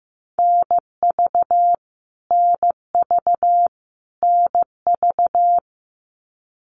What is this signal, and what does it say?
Morse code, heard as a keyed tone, about 15 wpm, 700 Hz. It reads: NV NV NV
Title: US States Repeated 3x 15wpm